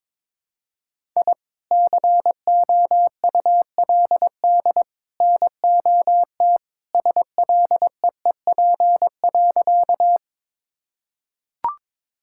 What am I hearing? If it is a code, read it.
Morse code, heard as a keyed tone, about 21 wpm, 700 Hz. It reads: I COULD NOT SLEEP.
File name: practice / 22wpm / Sentences with 3-Word Phrases - 22wpm